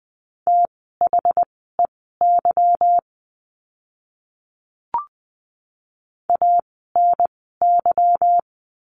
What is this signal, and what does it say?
Morse code, heard as a keyed tone, about 20 wpm, spaced as for 14 wpm, 700 Hz. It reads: THEY ANY
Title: 160 Lesson 26f - Letter g - Words 20wpm